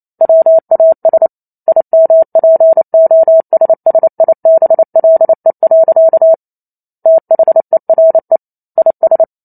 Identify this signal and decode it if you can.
Morse code, heard as a keyed tone, about 28 wpm, 650 Hz. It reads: WAS IMPOSSIBLE. THERE IS
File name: Chapter 2 - 28 WPM